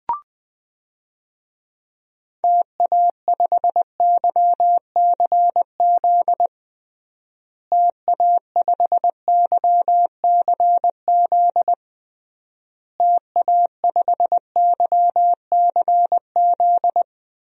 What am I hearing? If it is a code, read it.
Morse code, heard as a keyed tone, about 20 wpm, 700 Hz. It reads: TA5YCZ TA5YCZ TA5YCZ
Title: DXCC-100 Call Signs - 6 Characters - Repeated 3x - 20wpm